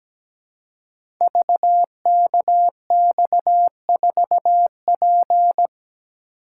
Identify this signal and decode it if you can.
Morse code, heard as a keyed tone, about 17 wpm, 700 Hz. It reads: VKX4P
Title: Code Groups - 5 Letter:Numbers - 17wpm